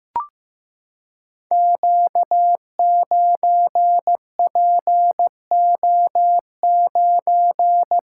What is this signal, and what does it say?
Morse code, heard as a keyed tone, about 15 wpm, 700 Hz. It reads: Q9PO9
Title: Code Groups - 5 Letter:Numbers - 15wpm